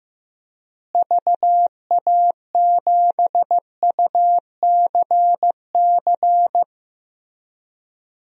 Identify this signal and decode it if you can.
Morse code, heard as a keyed tone, about 15 wpm, 700 Hz. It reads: VA7UCC
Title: DXCC-100 Call Signs - 6 Characters - Repeated 2x - 15wpm